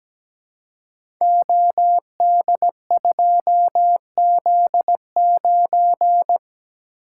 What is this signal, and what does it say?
Morse code, heard as a keyed tone, about 17 wpm, 700 Hz. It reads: OD2Z9
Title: Code Groups - 5 Letter:Numbers - 17wpm